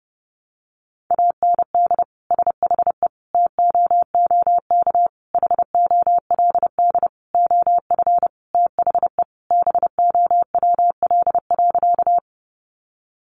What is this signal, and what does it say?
Morse code, heard as a keyed tone, about 30 wpm, 700 Hz. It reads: AND SHE TOOK HOLD OF THE BOWL.
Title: Sentences from Top 1200 Words 30wpm